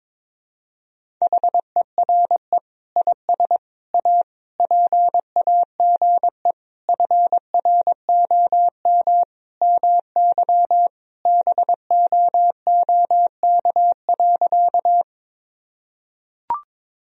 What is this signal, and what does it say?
Morse code, heard as a keyed tone, about 22 wpm, 700 Hz. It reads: HERE IS A PAGE FROM MY BOOK.
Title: Sentences from Top 1000 Words - Review Part 1 - 22wpm